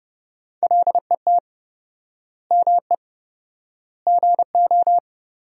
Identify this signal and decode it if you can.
Morse code, heard as a keyed tone, about 30 wpm, 700 Hz. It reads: LET ME GO
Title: 4x Word-Spacing - Sentences from Top 300 Words - 30wpm